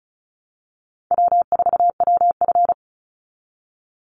Morse code, heard as a keyed tone, about 34 wpm, 700 Hz. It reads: W4WF
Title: Easy Call Signs Repeated 2x - 35wpm